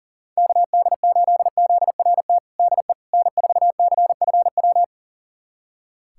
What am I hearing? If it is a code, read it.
Morse code, heard as a keyed tone, about 40 wpm, 700 Hz. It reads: KD8ZRT DE N4CFW
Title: USA Callsign DE Callsign 40wpm